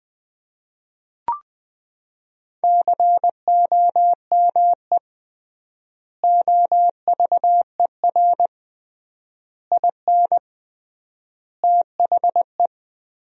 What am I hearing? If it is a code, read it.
Morse code, heard as a keyed tone, about 20 wpm, 700 Hz. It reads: COME OVER IN THE
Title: 3x Word-Spacing - Sentences from Top 300 Words - 20wpm